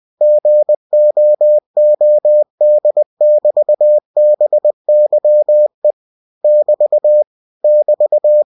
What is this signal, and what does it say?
Morse code, heard as a keyed tone, about 20 wpm, 600 Hz. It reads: GOOD<BT>BYE <BT> <BT>
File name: cw-books / winnie-the-pooh / winnie-20wpm-page-0101